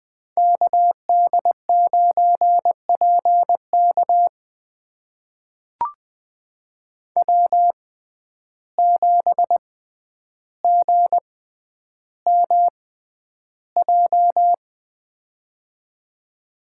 Morse code, heard as a keyed tone, about 20 wpm, spaced as for 10 wpm, 700 Hz. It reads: KD9PK W7GMJ
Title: USA Call Signs - 5 Characters - 6x Character Spacing - 20wpm